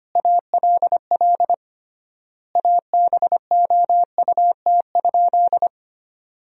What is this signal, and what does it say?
Morse code, heard as a keyed tone, about 25 wpm, 700 Hz. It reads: ALL ABOUT?
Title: 3x Word-Spacing - Sentences from Top 100 Words - 25wpm